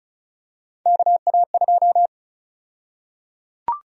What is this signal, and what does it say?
Morse code, heard as a keyed tone, about 34 wpm, 700 Hz. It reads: KA2
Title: USA Call Sign Prefixes - Repeated 2x - 35wpm